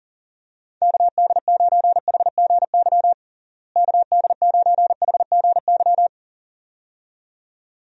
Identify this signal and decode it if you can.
Morse code, heard as a keyed tone, about 40 wpm, 700 Hz. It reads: KD9HGY KD9HGY
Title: USA Call Signs - 6 Characters - Repeated 2x - 40wpm